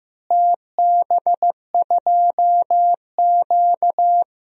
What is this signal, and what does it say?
Morse code, heard as a keyed tone, about 15 wpm, 700 Hz. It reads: TB2Q
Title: DXCC-100 Call Signs - 4 Characters - Repeated 3x - 15wpm